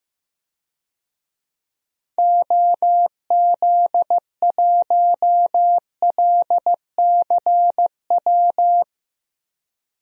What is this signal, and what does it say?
Morse code, heard as a keyed tone, about 15 wpm, 700 Hz. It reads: OZ1LCW
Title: DXCC-100 Call Signs - 6 Characters - Repeated 2x - 15wpm